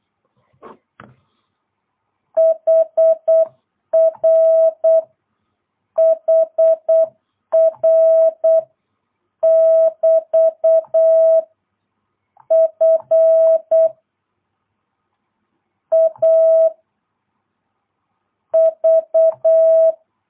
Morse code, heard as a keyed tone, about 8 wpm, 660 Hz. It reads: HR HR <BT> F A V
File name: jh8cbh / 2022cw